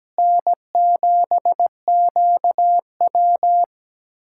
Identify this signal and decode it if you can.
Morse code, heard as a keyed tone, about 17 wpm, 700 Hz. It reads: N7QW
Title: USA Callsign DE Callsign - 4 Characters - 17wpm